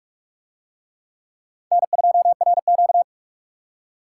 Morse code, heard as a keyed tone, about 44 wpm, 700 Hz. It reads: N2RQ